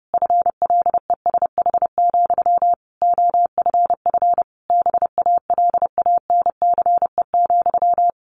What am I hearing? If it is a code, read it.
Morse code, heard as a keyed tone, about 30 wpm, 700 Hz. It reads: FLESH, OFF BALANCE,
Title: A Sound Of Thunder 30wpm